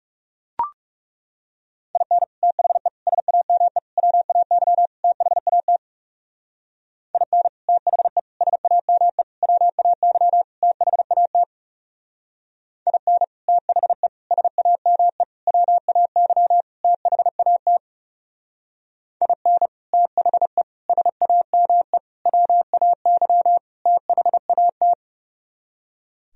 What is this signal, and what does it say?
Morse code, heard as a keyed tone, about 34 wpm, 700 Hz. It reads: IN THE SAME WAY THAT IN THE SAME WAY THAT IN THE SAME WAY THAT IN THE SAME WAY THAT
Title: Speed-Racing - Sets of 5 Words 30wpm